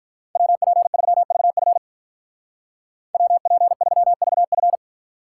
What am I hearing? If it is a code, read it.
Morse code, heard as a keyed tone, about 43 wpm, 700 Hz. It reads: WP3VF WP3VF
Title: Speed Racing - USA 2x2 Call Signs 35wpm